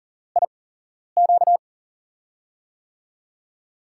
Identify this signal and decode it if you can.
Morse code, heard as a keyed tone, about 38 wpm, spaced as for 11 wpm, 700 Hz. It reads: IQ
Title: USA Call Signs - 5 Characters - 8x Character Spacing - 40wpm